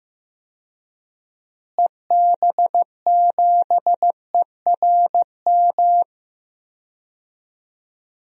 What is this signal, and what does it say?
Morse code, heard as a keyed tone, about 15 wpm, 700 Hz. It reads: EB7ERM
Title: DXCC-100 Call Signs - 6 Characters - Repeated 2x - 15wpm